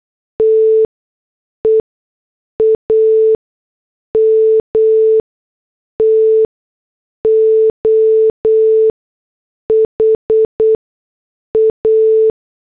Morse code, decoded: TEAM TO HA